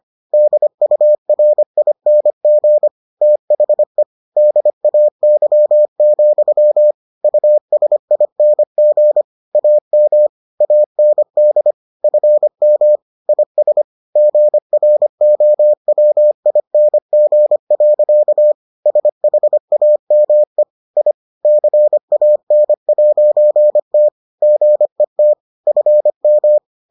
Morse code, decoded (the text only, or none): DURING THE DAY, USING AM AND FM IS GROWING. SHAME I CAN'T GET FM